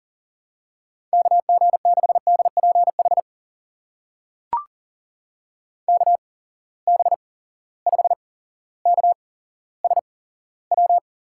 Morse code, heard as a keyed tone, about 39 wpm, spaced as for 17 wpm, 700 Hz. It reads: KG6DPH KB5KSW
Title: USA Call Signs - 6 Characters - 8x Character Spacing - 40wpm